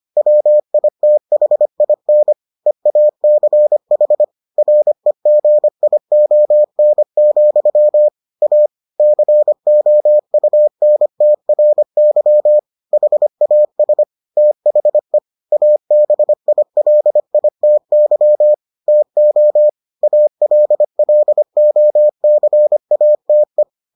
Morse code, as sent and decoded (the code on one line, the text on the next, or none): .-- .. - .... .. -. / . .- -.-. .... / .-. --. .. --- -. --..-- / .- / -.-. --- ..- -. - .-. -.-- / .... .- ... / - .... / .- -... .. .-.. .. - -.-- / - --- / .- .-.. .-.. --- -.-. .- - .
WITHIN EACH REGION, A COUNTRY HAS THE ABILITY TO ALLOCATE